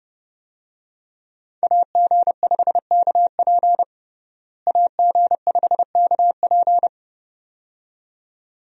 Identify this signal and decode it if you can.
Morse code, heard as a keyed tone, about 30 wpm, 700 Hz. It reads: AG5KP AG5KP
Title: USA Call Signs - 5 Characters - Repeated 2x - 30wpm